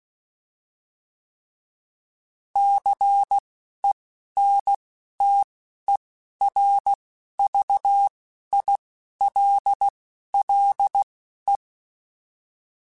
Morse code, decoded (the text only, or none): CENTERVILLE